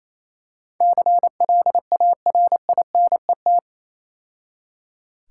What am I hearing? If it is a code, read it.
Morse code, heard as a keyed tone, about 26 wpm, 700 Hz. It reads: CLARINET
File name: Easy Words - Speed-Racing with Hints - 22wpm